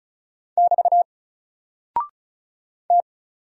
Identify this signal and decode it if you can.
Morse code, heard as a keyed tone, about 35 wpm, 700 Hz. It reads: <BT> T
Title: Single Character (Letter:Number:Punctuation:Prosign) 35wpm